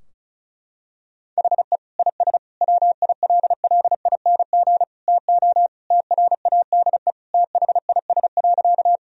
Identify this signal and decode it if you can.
Morse code, heard as a keyed tone, about 36 wpm, 700 Hz. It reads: HE IS WILLING TO TRADE THIS.
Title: Sentences from Top 1000 Words - Review Part 4 - 35wpm